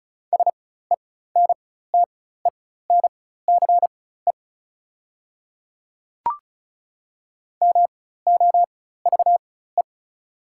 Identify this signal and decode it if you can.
Morse code, heard as a keyed tone, about 36 wpm, spaced as for 16 wpm, 700 Hz. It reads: SENTENCE MOVE
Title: 4x Character Spacing - Top 300 Words - 35wpm